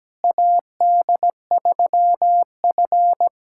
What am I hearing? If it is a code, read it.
Morse code, heard as a keyed tone, about 17 wpm, 700 Hz. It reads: AD3F
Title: USA Callsign DE Callsign - 4 Characters - 17wpm